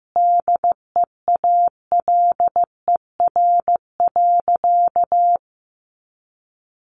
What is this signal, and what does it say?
Morse code, heard as a keyed tone, about 15 wpm, 700 Hz. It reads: DEALER.
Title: Sentences from Top 1100 Words 15wpm